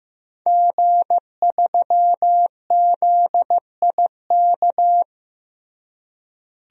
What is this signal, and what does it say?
Morse code, heard as a keyed tone, about 14 wpm, 700 Hz. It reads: G3ZIK
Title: DXCC-100 Call Signs - 5 Characters - Repeated 2x - 15wpm